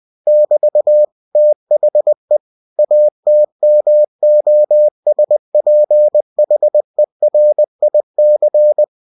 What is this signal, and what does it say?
Morse code, heard as a keyed tone, about 20 wpm, 600 Hz. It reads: <BT> THE ATMOSPHERIC
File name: cw-books / winnie-the-pooh / winnie-20wpm-page-0096